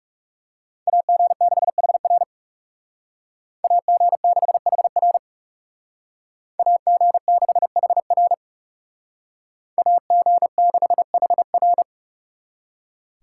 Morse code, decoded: AG6HR AG6HR AG6HR AG6HR

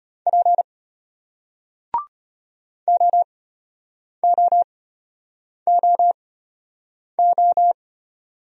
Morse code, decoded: P OOOO

